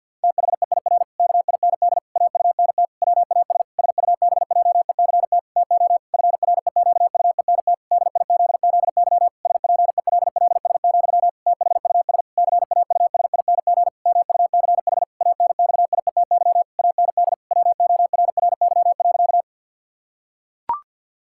Morse code, heard as a keyed tone, about 50 wpm, 700 Hz. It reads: THEIR KIND AUNT WAS SUBJECT TO FREQUENT DIZZY SPELLS, THUS CAUSING MUCH ANXIETY AND WORRY.
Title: Pangram Challenge v2 - 50wpm